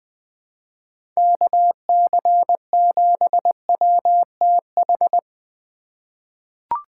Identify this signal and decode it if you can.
Morse code, heard as a keyed tone, about 21 wpm, 700 Hz. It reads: KC7WTH
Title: USA Call Signs - 6 Characters - 2x Character Spacing - 20wpm